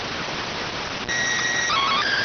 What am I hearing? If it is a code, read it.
SSTV (slow-scan television), PD 290.